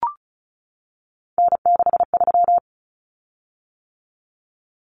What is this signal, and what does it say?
Morse code, heard as a keyed tone, about 34 wpm, 700 Hz. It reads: N63
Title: Common Cut Numbers - 3-Digit 35wpm